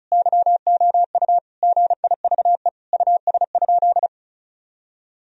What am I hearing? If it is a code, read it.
Morse code, heard as a keyed tone, about 36 wpm, 700 Hz. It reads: YOU GIVE US?
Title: Sentences from Top 1000 Words - Review Part 4 - 35wpm